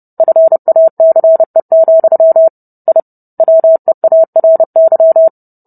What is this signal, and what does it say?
Morse code, heard as a keyed tone, about 30 wpm, 650 Hz. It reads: FACE, I WEARY